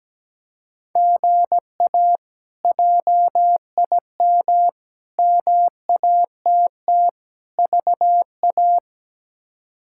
Morse code, heard as a keyed tone, about 17 wpm, 700 Hz. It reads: GA JIM MATT VA
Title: K1USN SST QSO Practice - 17wpm